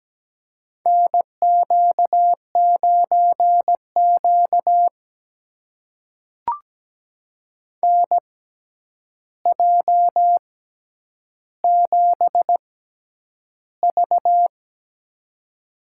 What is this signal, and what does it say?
Morse code, heard as a keyed tone, about 17 wpm, spaced as for 9 wpm, 700 Hz. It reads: NQ9Q NJ7V
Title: USA Call Signs - 4 Characters - 6x Character Spacing - 17wpm